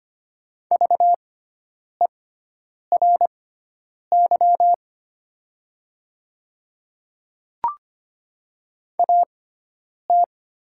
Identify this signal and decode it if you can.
Morse code, heard as a keyed tone, about 25 wpm, spaced as for 9 wpm, 700 Hz. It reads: VERY AT
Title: 6x Character Spacing - Top 300 Words - 25wpm